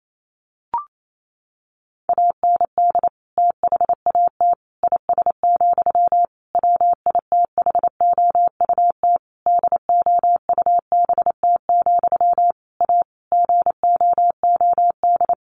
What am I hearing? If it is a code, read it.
Morse code, heard as a keyed tone, about 28 wpm, 700 Hz. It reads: AND THAT IS, WITHOUT DOUBT, A GOOD
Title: Sentences from Top 1500 Words 28wpm